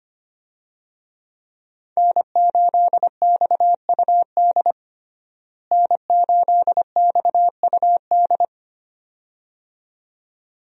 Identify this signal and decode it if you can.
Morse code, heard as a keyed tone, about 25 wpm, 700 Hz. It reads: N8XUD N8XUD
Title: USA Call Signs - 5 Characters - Repeated 2x - 25wpm